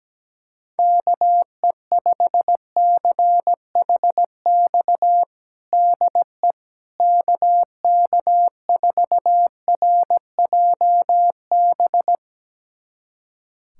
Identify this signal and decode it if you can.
Morse code, heard as a keyed tone, about 17 wpm, 700 Hz. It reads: KE5CHX DE KK4RJB
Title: USA Callsign DE Callsign - 6 Characters - 17wpm